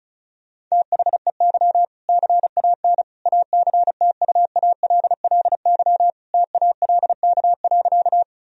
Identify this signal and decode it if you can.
Morse code, heard as a keyed tone, about 36 wpm, 700 Hz. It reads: THEY CAN ACTUALLY TALK.
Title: Sentences from Top 1000 Words - Review Part 4 - 35wpm